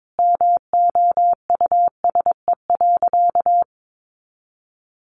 Morse code, heard as a keyed tone, about 22 wpm, 700 Hz. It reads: MOUSE.
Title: A Sound Of Thunder 22wpm